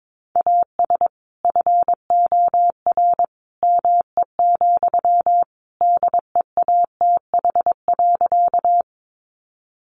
Morse code, heard as a keyed tone, about 22 wpm, 700 Hz. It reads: AS FOR ME, DEATH.
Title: Sentences from Top 1100 Words 22wpm